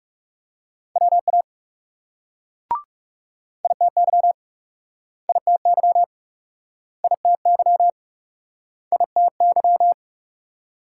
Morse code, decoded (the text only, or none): WA ITY ITY ITY ITY